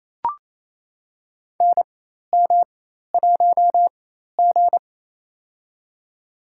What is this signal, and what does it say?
Morse code, heard as a keyed tone, about 28 wpm, spaced as for 13 wpm, 700 Hz. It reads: NM1G
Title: USA Call Signs - 4 Characters - 4x Character Spacing - 28wpm